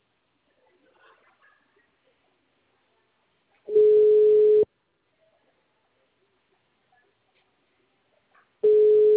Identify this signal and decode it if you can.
Ringback tone (Europe, 425 Hz).